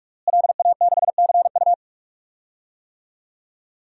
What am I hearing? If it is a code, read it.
Morse code, heard as a keyed tone, about 45 wpm, 700 Hz. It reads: LA6CU
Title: DXCC-100 Call Signs - 5 Characters - Repeated 3x - 45wpm